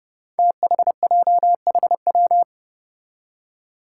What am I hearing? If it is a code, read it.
Morse code, heard as a keyed tone, about 31 wpm, 700 Hz. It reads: THJHW
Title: Five Challenging Letter-Number Combinations 30wpm